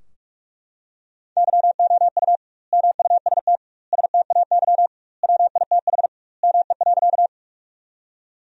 Morse code, heard as a keyed tone, about 41 wpm, 700 Hz. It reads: YOU MUST STAY WITH ME.
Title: 2x Word-Spacing - Sentences from Top 500 Words - 45wpm